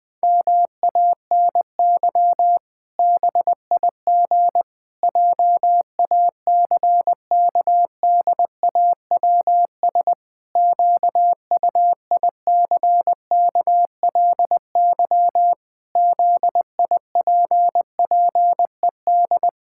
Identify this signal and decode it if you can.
Morse code, heard as a keyed tone, about 20 wpm, 700 Hz. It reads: MANY BIG JACKDAWS QUICKLY ZIPPED
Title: Pangram Challenge v2 - 20wpm